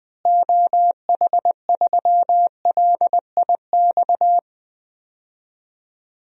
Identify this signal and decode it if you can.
Morse code, heard as a keyed tone, about 20 wpm, 700 Hz. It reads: OH3LIX
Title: DXCC-100 Call Signs - 6 Characters - Repeated 2x - 20wpm